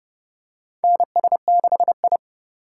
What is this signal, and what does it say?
Morse code, heard as a keyed tone, about 30 wpm, 700 Hz. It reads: NS6I
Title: USA Call Signs - 3x Character Spacing - 30wpm